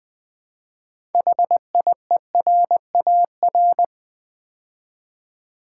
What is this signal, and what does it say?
Morse code, heard as a keyed tone, about 20 wpm, 700 Hz. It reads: HIERAR
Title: Build A Word - 3 Letters At A Time - 9-Letter Words - 20wpm